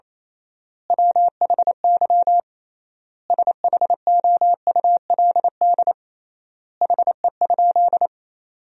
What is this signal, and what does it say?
Morse code, heard as a keyed tone, about 28 wpm, 700 Hz. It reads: WHY SHOULD HE?
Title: 3x Word-Spacing - Sentences from Top 300 Words - 28wpm